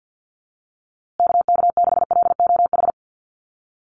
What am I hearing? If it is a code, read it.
Morse code, heard as a keyed tone, about 50 wpm, 700 Hz. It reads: KK6LOH
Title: USA Call Signs Repeated 2x 50wpm